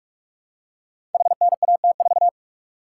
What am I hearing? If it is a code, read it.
Morse code, heard as a keyed tone, about 44 wpm, 700 Hz. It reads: HNAT4
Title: Code Groups - 5 Letter:Numbers - 45wpm